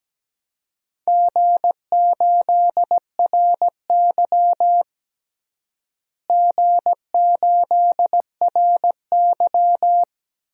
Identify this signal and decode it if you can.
Morse code, heard as a keyed tone, about 17 wpm, 700 Hz. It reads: G8RY G8RY
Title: DXCC-100 Call Signs - 4 Characters - Repeated 3x - 17wpm